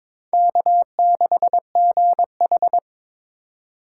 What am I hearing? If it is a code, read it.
Morse code, heard as a keyed tone, about 22 wpm, 700 Hz. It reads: K6GH
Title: USA Call Signs - 4 Characters - 3x Character Spacing - 22wpm